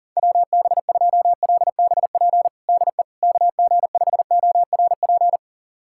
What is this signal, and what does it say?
Morse code, heard as a keyed tone, about 40 wpm, 700 Hz. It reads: WB2LBP DE KG5ORP
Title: USA Callsign DE Callsign - 6 Characters - 40wpm